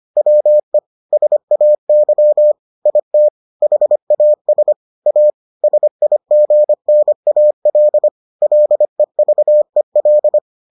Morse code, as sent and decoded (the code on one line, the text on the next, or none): .-- . / ... .- -.-- / .. - / .... .- ... / .- / ... .. --. -. .- .-.. / .-.. ...- . .-..
WE SAY IT HAS A SIGNAL LEVEL